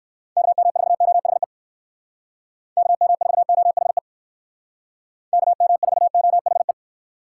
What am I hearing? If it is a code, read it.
Morse code, heard as a keyed tone, about 59 wpm, 700 Hz. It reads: KM4OHE KM4OHE KM4OHE